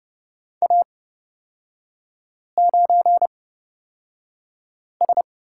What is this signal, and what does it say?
Morse code, heard as a keyed tone, about 31 wpm, spaced as for 5 wpm, 690 Hz. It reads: A9S
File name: course / 30:10wpm / 236 Lesson 38e - Letter q - Three Characters 30wpm